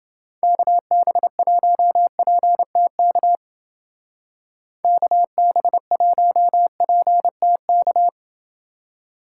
Morse code, heard as a keyed tone, about 28 wpm, 700 Hz. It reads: KB1PTK KB1PTK